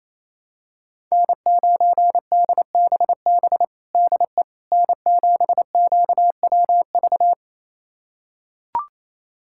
Morse code, heard as a keyed tone, about 28 wpm, 700 Hz. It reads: N9DBB DE N7QWV
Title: USA Callsign DE Callsign - 5 Characters - 28wpm